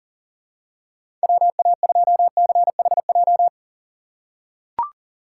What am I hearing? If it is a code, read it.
Morse code, heard as a keyed tone, about 40 wpm, 700 Hz. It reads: WA2CHJ
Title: USA Call Signs - 6 Characters - 4x Character Spacing - 40wpm